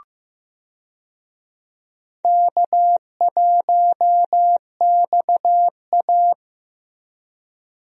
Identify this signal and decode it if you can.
Morse code, heard as a keyed tone, about 16 wpm, 700 Hz. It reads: K1XA
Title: USA Call Signs - 4 Characters - Repeated 2x - 15wpm